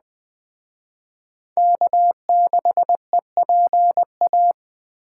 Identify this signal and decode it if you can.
Morse code, heard as a keyed tone, about 19 wpm, 700 Hz. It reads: K6EPA